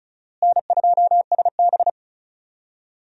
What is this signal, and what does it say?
Morse code, heard as a keyed tone, about 36 wpm, 700 Hz. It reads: N2SB
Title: USA Call Signs - 4 Characters - Repeated 2x - 35wpm